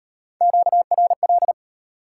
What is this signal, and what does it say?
Morse code, heard as a keyed tone, about 32 wpm, 690 Hz. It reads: QRL